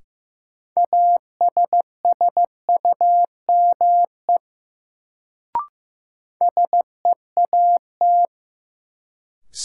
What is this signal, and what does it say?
Morse code, heard as a keyed tone, about 15 wpm, 700 Hz. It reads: ASSUME SEAT
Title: Sentences from Top 1500 Words - New Word Drill Alt - 15wpm